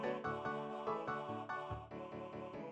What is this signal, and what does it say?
There is no code.